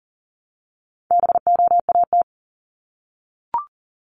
Morse code, heard as a keyed tone, about 38 wpm, 700 Hz. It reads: BOAT
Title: Multi-letter Phonemes Repeated 3x 40wpm